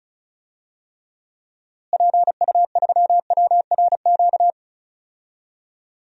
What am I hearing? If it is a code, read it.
Morse code, heard as a keyed tone, about 36 wpm, 700 Hz. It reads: PU3WRQ
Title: DXCC-100 Call Signs - 6 Characters - Repeated 2x - 35wpm